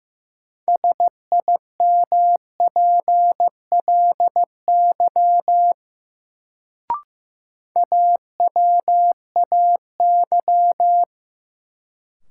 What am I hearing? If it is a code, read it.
Morse code, heard as a keyed tone, about 15 wpm, 700 Hz. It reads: SIMPLY AWAY
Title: Sentences from Top 600 Words - New Word Drill - 15wpm